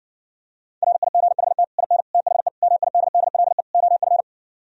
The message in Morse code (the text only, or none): RIGHT IN THE MIDDLE OF